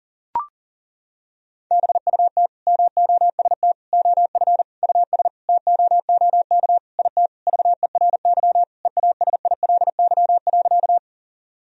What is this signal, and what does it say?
Morse code, heard as a keyed tone, about 40 wpm, 700 Hz. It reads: BUT MOST OF US TOOK IT VERY EASILY.
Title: Sentences from Top 1000 Words - Review Part 1 - 40wpm